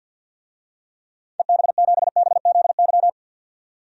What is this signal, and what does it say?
Morse code, heard as a keyed tone, about 52 wpm, 700 Hz. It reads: EB7BZY
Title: DXCC-100 Call Signs - 6 Characters - 50wpm